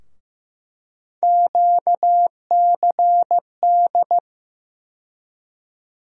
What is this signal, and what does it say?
Morse code, heard as a keyed tone, about 15 wpm, 700 Hz. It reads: QCD
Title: USA Call Sign Postfixes - 3 Letters 15wpm